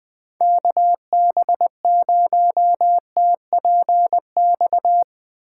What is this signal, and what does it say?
Morse code, heard as a keyed tone, about 20 wpm, 700 Hz. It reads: KB0TPX